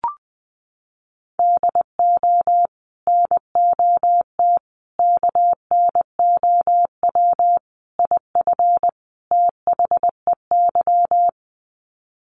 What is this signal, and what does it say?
Morse code, heard as a keyed tone, about 20 wpm, 700 Hz. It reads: DO NOT KNOW IF THEY
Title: Sets of 5 Words Encore 20wpm